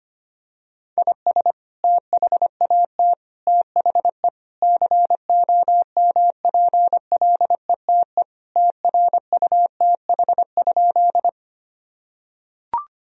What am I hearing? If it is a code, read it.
Morse code, heard as a keyed tone, about 25 wpm, 700 Hz. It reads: IS THAT THE COMPLETE TRUTH?
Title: Sentences from Top 1000 Words - Review Part 2 - 25wpm